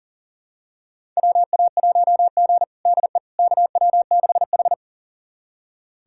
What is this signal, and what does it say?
Morse code, heard as a keyed tone, about 40 wpm, 700 Hz. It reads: WA1G DE KW6H
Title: USA Callsign DE Callsign - 4 Characters - 40wpm